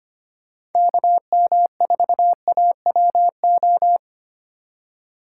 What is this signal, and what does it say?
Morse code, heard as a keyed tone, about 32 wpm, 700 Hz. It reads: KM4AWO